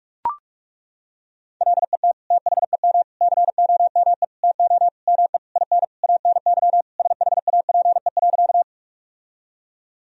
Morse code, heard as a keyed tone, about 41 wpm, 700 Hz. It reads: LET THEM COME TO ME IN ANY SHAPE.
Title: Sentences from Top 1000 Words - Review Part 3 - 45wpm